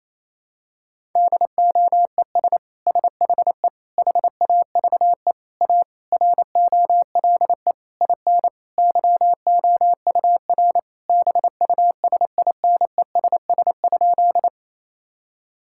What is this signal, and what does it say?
Morse code, heard as a keyed tone, about 27 wpm, 700 Hz. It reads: DOES SHE HAVE A ROLE IN YOUR BUSINESS?
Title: Sentences from Top 1000 Words - Review Part 6 - 28wpm